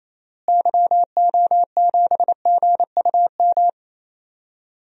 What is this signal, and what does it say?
Morse code, heard as a keyed tone, about 28 wpm, 700 Hz. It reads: YO7GUM
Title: DXCC-100 Call Signs - 6 Characters - Repeated 2x - 28wpm